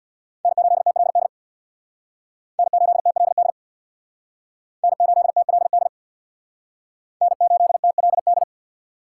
Morse code, heard as a keyed tone, about 57 wpm, 700 Hz. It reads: N8TLD N8TLD N8TLD N8TLD
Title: Speed Racing - USA 1x3 Call Signs 50wpm